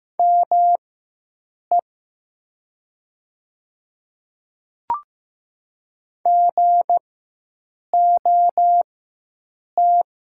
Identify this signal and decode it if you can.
Morse code, heard as a keyed tone, about 15 wpm, spaced as for 7 wpm, 700 Hz. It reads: ME GOT
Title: 4x Character Spacing - Top 300 Words - 15wpm